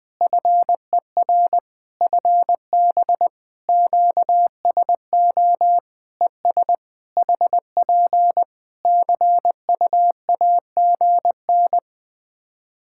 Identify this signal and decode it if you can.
Morse code, heard as a keyed tone, about 20 wpm, 700 Hz. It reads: FER FB QSO ES HP CUAGN
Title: K1USN SST QSO Practice 20wpm